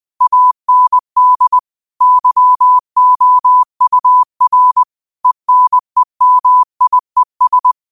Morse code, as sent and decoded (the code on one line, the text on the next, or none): .- -. -.. / -.-- --- ..- .-. / . -. -- .. ...
AND YOUR ENEMIES